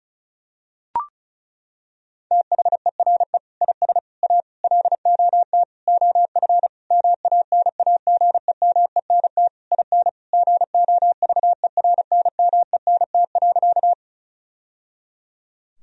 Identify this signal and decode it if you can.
Morse code, heard as a keyed tone, about 35 wpm, 700 Hz. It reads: THERE IS A LOT OF MANAGEMENT IN GOVERNMENT.
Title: Sentences from Top 1000 Words - Review Part 4 - 35wpm